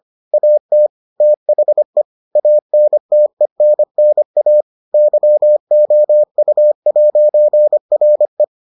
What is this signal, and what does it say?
Morse code, heard as a keyed tone, about 25 wpm, 600 Hz. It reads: AT THE ANTENNA YOU'RE